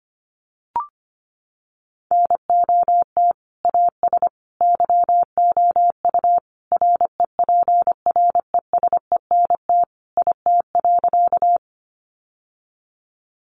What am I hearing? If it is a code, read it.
Morse code, heard as a keyed tone, about 25 wpm, 700 Hz. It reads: NOT AS YOU REPRESENT IT.
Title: Sentences from Top 1200 Words 25wpm